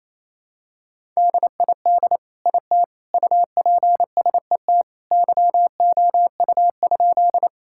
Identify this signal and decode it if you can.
Morse code, heard as a keyed tone, about 28 wpm, 700 Hz. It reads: DID IT UPSET YOU?
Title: Sentences from Top 1000 Words - Review Part 5 - 28wpm